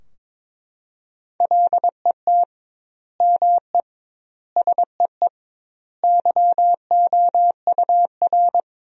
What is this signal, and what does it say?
Morse code, heard as a keyed tone, about 22 wpm, 700 Hz. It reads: LET ME SEE YOUR
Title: 2x Word-Spacing - Sentences from Top 500 Words - 22wpm